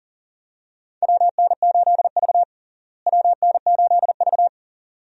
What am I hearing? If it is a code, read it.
Morse code, heard as a keyed tone, about 40 wpm, 700 Hz. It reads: WN8V WN8V
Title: USA Call Signs - 4 Characters - Repeated 2x - 40wpm